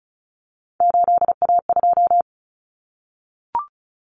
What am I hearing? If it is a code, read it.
Morse code, heard as a keyed tone, about 34 wpm, 700 Hz. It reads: 8A2
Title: Common Cut Numbers - 3-Digit 35wpm